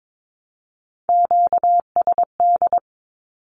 Morse code, heard as a keyed tone, about 22 wpm, 700 Hz. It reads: QSD
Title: Q-Codes Repeated 3x 22wpm